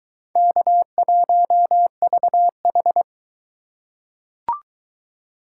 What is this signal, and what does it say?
Morse code, heard as a keyed tone, about 22 wpm, 700 Hz. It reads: K1VH